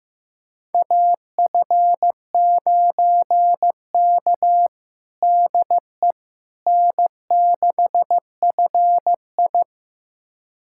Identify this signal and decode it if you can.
Morse code, heard as a keyed tone, about 15 wpm, 700 Hz. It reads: AF9K DE N6FI
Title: USA Callsign DE Callsign - 4 Characters - 15wpm